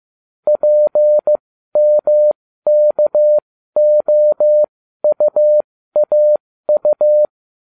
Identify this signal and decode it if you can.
Morse code, heard as a keyed tone, about 15 wpm, spaced as for 12 wpm, 600 Hz. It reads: PMKOUAU